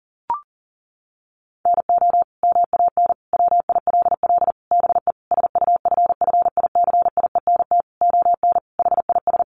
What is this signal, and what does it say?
Morse code, heard as a keyed tone, about 40 wpm, 700 Hz. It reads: NO MAN WILL BE SUFFICIENT ON HIS
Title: Sentences from Top 1400 Words 40wpm